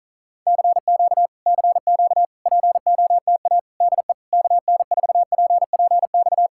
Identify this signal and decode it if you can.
Morse code, heard as a keyed tone, about 46 wpm, 700 Hz. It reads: CQ CQ POTA DE KN4PPX